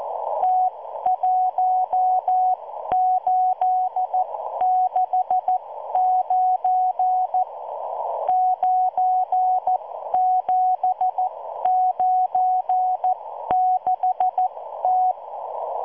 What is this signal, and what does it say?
Morse code, heard as a keyed tone, about 14 wpm, spaced as for 12 wpm, 740 Hz. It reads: T1869 9796T